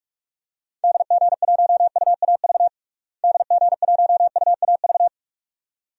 Morse code, heard as a keyed tone, about 45 wpm, 700 Hz. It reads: DG1UAV DG1UAV
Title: DXCC-100 Call Signs - 6 Characters - Repeated 2x - 45wpm